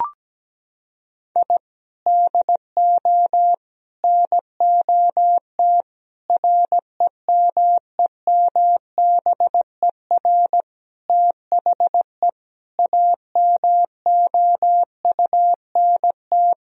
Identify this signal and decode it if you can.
Morse code, heard as a keyed tone, about 17 wpm, 700 Hz. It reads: I DO NOT REMEMBER THE AMOUNT
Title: Sentences from Top 1000 Words - Review Part 4 - 17wpm